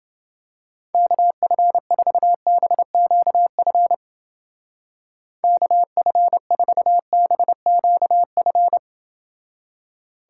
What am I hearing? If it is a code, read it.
Morse code, heard as a keyed tone, about 29 wpm, 700 Hz. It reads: KF4BQF KF4BQF